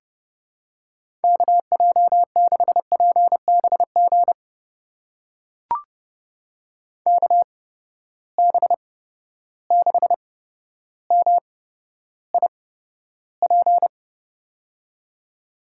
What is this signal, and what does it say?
Morse code, heard as a keyed tone, about 30 wpm, spaced as for 13 wpm, 700 Hz. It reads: KJ6PBG KB6MIP
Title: USA Call Signs - 6 Characters - 8x Character Spacing - 30wpm